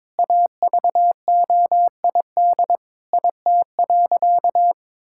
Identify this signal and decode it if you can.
Morse code, heard as a keyed tone, about 21 wpm, 700 Hz. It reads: AVOID IT.
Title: Sentences from Top 1000 Words - Review Part 6 - 22wpm